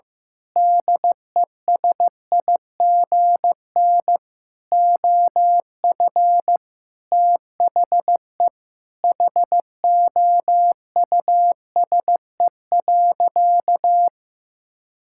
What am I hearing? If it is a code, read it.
Morse code, heard as a keyed tone, about 15 wpm, 700 Hz. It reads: DESIGN OF THE HOUSE.
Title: Sentences from Top 1000 Words - Review Part 6 - 15wpm